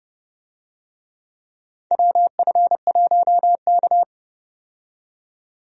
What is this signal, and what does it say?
Morse code, heard as a keyed tone, about 31 wpm, 700 Hz. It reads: WF1K
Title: USA Call Signs - 4 Characters - 30wpm